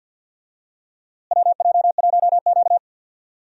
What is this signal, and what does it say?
Morse code, heard as a keyed tone, about 51 wpm, 700 Hz. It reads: WJ1Q
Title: USA Call Signs - 4 Characters - Repeated 2x - 50wpm